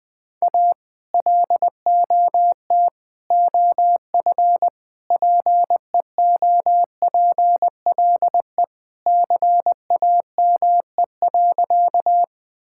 Morse code, heard as a keyed tone, about 21 wpm, 700 Hz. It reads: A LOT OF PEOPLE CAME.
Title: Sentences with 3-Word Phrases - 20wpm